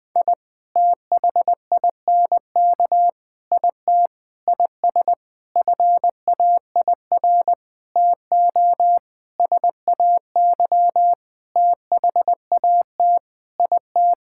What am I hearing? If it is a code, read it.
Morse code, heard as a keyed tone, about 20 wpm, 700 Hz. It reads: I THINK IT IS FAIR TO SAY THAT IT
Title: Sentences with 8-Word Phrases - 20wpm